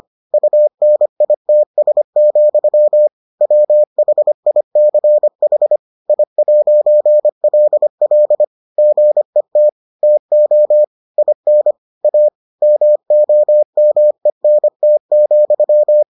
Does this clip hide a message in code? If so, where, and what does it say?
Yes: Morse code from 0.3 to 16.0 s, heard as a keyed tone, about 25 wpm, 600 Hz. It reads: UNITS, WHICH I'LL GET TO IN A MOMENT,